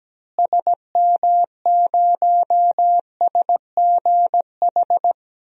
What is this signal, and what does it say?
Morse code, heard as a keyed tone, about 17 wpm, 700 Hz. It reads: SM0SGH
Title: DXCC-100 Call Signs - 6 Characters - Repeated 2x - 17wpm